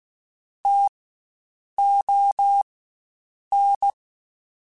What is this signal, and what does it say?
Morse code, heard as a keyed tone, about 16 wpm, spaced as for 7 wpm, 800 Hz. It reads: TON
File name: Spaced Alphas / Spaced Alphas mp3s / Ts sp @15